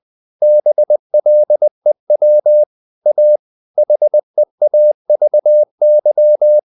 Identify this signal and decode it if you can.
Morse code, heard as a keyed tone, about 20 wpm, 600 Hz. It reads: BLEW A HEAVY
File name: the old man and the sea part 2 20wpm